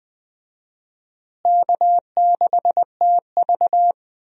Morse code, heard as a keyed tone, about 20 wpm, 700 Hz. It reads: K6TV
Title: USA Call Signs - 4 Characters - Repeated 3x - 20wpm